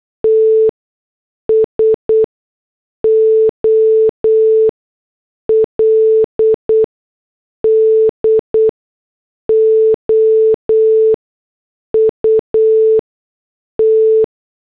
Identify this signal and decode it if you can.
Morse code, heard as a keyed tone, about 8 wpm, spaced as for 6 wpm, 440 Hz. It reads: T SOLD OUT